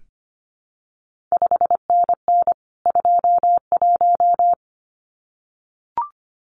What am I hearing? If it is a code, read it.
Morse code, heard as a keyed tone, about 25 wpm, 700 Hz. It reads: 5NN 21
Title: CQ WW DX Contest Exchange Encore - 25wpm